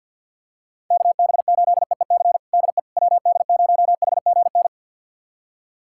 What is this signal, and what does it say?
Morse code, heard as a keyed tone, about 48 wpm, 700 Hz. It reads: KB8EEC DE WD0HGN
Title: USA Callsign DE Callsign - 6 Characters - 50wpm